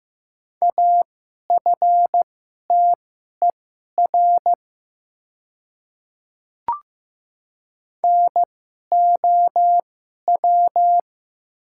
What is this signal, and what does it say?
Morse code, heard as a keyed tone, about 15 wpm, spaced as for 11 wpm, 700 Hz. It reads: AFTER NOW
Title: 2x Character Spacing - Top 200 Words - 15wpm